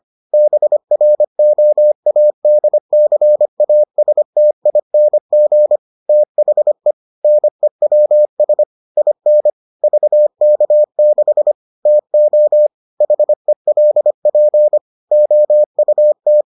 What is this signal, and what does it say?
Morse code, heard as a keyed tone, about 25 wpm, 600 Hz. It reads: BROADCASTING THE NEWS IN VK6 TO HELP OUT